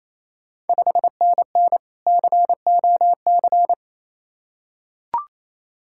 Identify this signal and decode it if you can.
Morse code, heard as a keyed tone, about 28 wpm, 690 Hz. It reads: 5NN COC
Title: Ontario QSO Party Exchange 28wpm